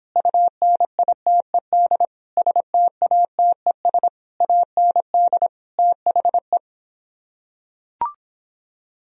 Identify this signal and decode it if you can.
Morse code, heard as a keyed tone, about 26 wpm, 700 Hz. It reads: UNITED STATES AND THE